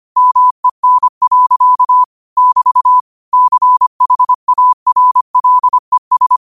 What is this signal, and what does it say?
Morse code, heard as a keyed tone, about 25 wpm, 1000 Hz. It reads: MEN. <BT> CHARLES
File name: qotdfiles / 25wpm